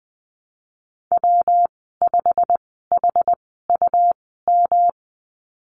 Morse code, heard as a keyed tone, about 20 wpm, spaced as for 15 wpm, 700 Hz. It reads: W5HUM
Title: 179 Lesson 29g - Slash - Call Signs 20wpm